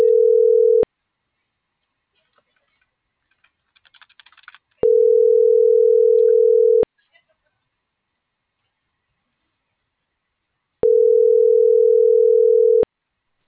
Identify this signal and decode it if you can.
Ringback tone (North America), 440 + 480 Hz.